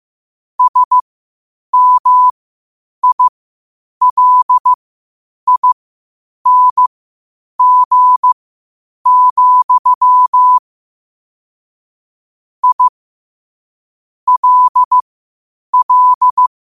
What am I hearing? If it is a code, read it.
Morse code, heard as a keyed tone, about 15 wpm, spaced as for 8 wpm, 1000 Hz. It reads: SMILING, I LL
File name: qotdfiles / 8wpm